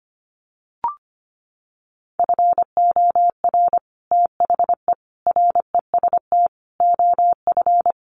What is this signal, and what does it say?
Morse code, heard as a keyed tone, about 25 wpm, 700 Hz. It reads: FOR THE REST OF
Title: Sets of 5 Words 25wpm